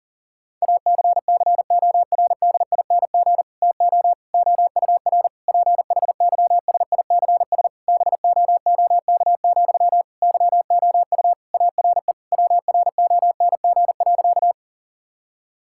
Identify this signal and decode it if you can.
Morse code, heard as a keyed tone, about 40 wpm, 700 Hz. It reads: ACCORDING TO OUR PHYSICS BOOK, YOU ARE WRONG.
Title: Sentences from Top 1000 Words - Review Part 5 - 40wpm